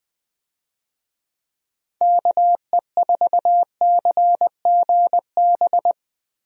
Morse code, heard as a keyed tone, about 21 wpm, 700 Hz. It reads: KE4CGB
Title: USA Call Signs - 6 Characters - Repeated 3x - 20wpm